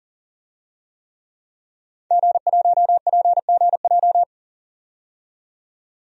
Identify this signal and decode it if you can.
Morse code, heard as a keyed tone, about 40 wpm, 700 Hz. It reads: G1PGJ
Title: DXCC-100 Call Signs - 5 Characters - 40wpm